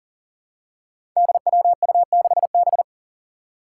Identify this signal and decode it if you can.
Morse code, heard as a keyed tone, about 38 wpm, 700 Hz. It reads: DWU6B
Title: Five Challenging Letter-Number Combinations 40wpm